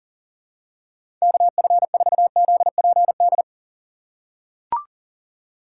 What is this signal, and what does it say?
Morse code, heard as a keyed tone, about 40 wpm, 700 Hz. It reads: KF4ZPD
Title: USA Call Signs - 6 Characters - 2x Character Spacing - 40wpm